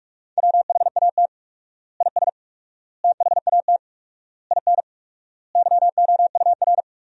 Morse code, heard as a keyed tone, about 46 wpm, 700 Hz. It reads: WHAT IS THAT IN YOUR
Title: 4x Word-Spacing - Sentences from Top 200 Words - 45wpm